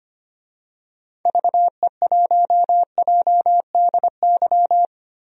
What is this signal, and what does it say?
Morse code, heard as a keyed tone, about 25 wpm, 700 Hz. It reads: VE1JDY
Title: DXCC-100 Call Signs - 6 Characters - Repeated 2x - 25wpm